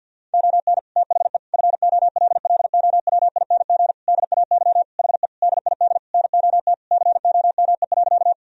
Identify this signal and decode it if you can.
Morse code, heard as a keyed tone, about 50 wpm, 700 Hz. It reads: ON THE FOLLOWING DAY HE DID NOT COME.
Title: Sentences from Top 1000 Words - Review Part 4 - 50wpm